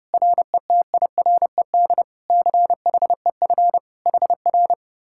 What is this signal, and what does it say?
Morse code, heard as a keyed tone, about 32 wpm, 700 Hz. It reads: RETIRED CHEF HR